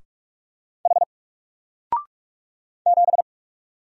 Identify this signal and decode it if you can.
Morse code, heard as a keyed tone, about 44 wpm, 700 Hz. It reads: H 7
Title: Single Letter-Number Alt 45wpm